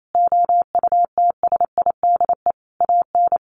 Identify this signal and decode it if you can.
Morse code, heard as a keyed tone, about 28 wpm, 700 Hz. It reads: OUTSIDE AN
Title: Sentences from Top 1100 Words 28wpm